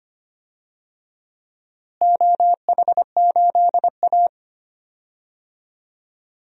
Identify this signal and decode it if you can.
Morse code, heard as a keyed tone, about 25 wpm, 700 Hz. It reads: OH8A
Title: DXCC-100 Call Signs - 4 Characters - Repeated 2x - 25wpm